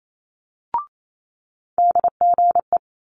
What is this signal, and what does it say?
Morse code, heard as a keyed tone, about 28 wpm, 700 Hz. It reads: DGE
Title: Multi-letter Phonemes Quick Fire 28wpm